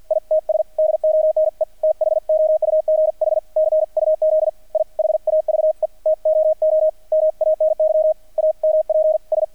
Morse code, heard as a keyed tone, about 48 wpm, 620 Hz, with radio noise. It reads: ITS DONE THOUGH CUZ I HAVE TOO MANY AMPS
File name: archived / CRSnet